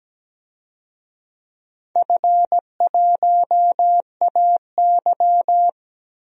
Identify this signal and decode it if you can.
Morse code, heard as a keyed tone, about 17 wpm, 700 Hz. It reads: F1AY
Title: DXCC-100 Call Signs - 4 Characters - Repeated 3x - 17wpm